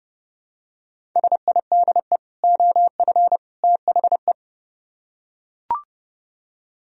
Morse code, heard as a keyed tone, about 30 wpm, 700 Hz. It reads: SIDE OF THE